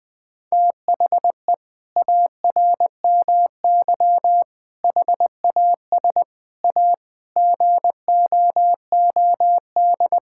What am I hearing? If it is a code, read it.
Morse code, heard as a keyed tone, about 21 wpm, 700 Hz. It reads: THE ARMY HAS A GOOD
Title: Sentences from Top 1000 Words - Review Part 5 - 20wpm